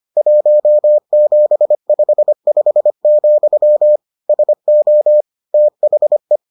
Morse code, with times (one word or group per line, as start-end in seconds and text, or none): 0.2-4.0 1755,
4.3-5.2 SO
5.5-6.4 THE